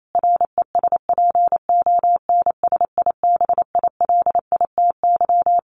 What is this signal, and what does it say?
Morse code, heard as a keyed tone, about 28 wpm, 700 Hz. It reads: RESPONSIBILITY
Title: Sentences from Top 1200 Words 28wpm